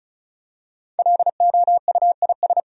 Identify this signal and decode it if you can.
Morse code, heard as a keyed tone, about 36 wpm, 700 Hz. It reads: LOUIS
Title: USA Common Names - Max 5 Letters - 35wpm